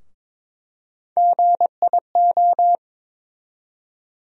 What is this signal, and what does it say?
Morse code, heard as a keyed tone, about 22 wpm, 700 Hz. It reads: GIO
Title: USA Call Sign Postfixes - 3 Letters 22wpm